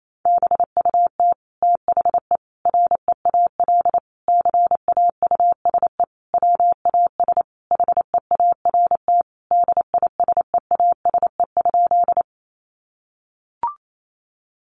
Morse code, heard as a keyed tone, about 28 wpm, 700 Hz. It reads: BUT THE REAL CAUSE WAS HEART DISEASE?
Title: Sentences from Top 1200 Words 28wpm